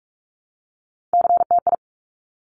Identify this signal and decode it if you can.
Morse code, heard as a keyed tone, about 46 wpm, 700 Hz. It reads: CTI
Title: Top Three Letter Combinations in Words 45wpm